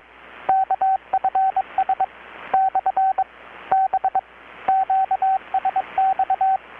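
Morse code, heard as a keyed tone, about 23 wpm, 770 Hz, with radio noise. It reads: KFS / B QSX
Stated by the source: Morse from KPH's Night of Nights XX maritime radiotelegraphy event, received in Chiapas, Mexico about 3,800 km away on a Kenwood R-600 with a grounded 10 m long-wire antenna.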